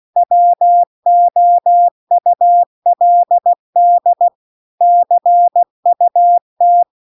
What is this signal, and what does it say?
Morse code, heard as a keyed tone, about 16 wpm, 700 Hz. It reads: WOULD CUT
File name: CW-The-Big-Twoheaded-River-2-16WPM